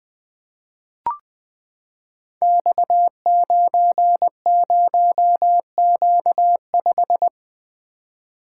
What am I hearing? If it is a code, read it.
Morse code, heard as a keyed tone, about 20 wpm, 700 Hz. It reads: X90Q5
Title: Code Groups - 5 Letter:Numbers - 20wpm